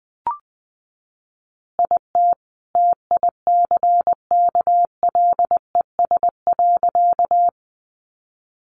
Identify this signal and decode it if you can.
Morse code, heard as a keyed tone, about 20 wpm, 700 Hz. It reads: IT TICKLES.
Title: Additional Easy Phrases 20wpm